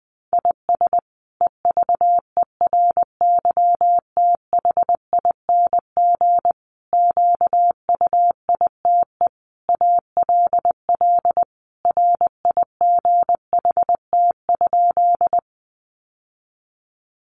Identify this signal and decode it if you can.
Morse code, heard as a keyed tone, about 20 wpm, 700 Hz. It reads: IS EVERYTHING QUITE ALL RIGHT?
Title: Sentences from Top 700 Words 20wpm